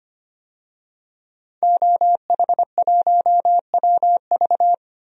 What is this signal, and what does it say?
Morse code, heard as a keyed tone, about 25 wpm, 700 Hz. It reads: OH1WV
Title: DXCC-100 Call Signs - 5 Characters - 25wpm